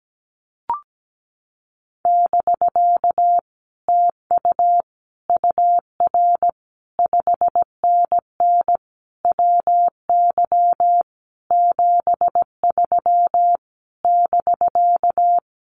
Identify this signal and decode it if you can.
Morse code, heard as a keyed tone, about 17 wpm, 700 Hz. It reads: <BK> TU UR 5NN WY 73 <BK>
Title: Common QSO Phrases Encore 17wpm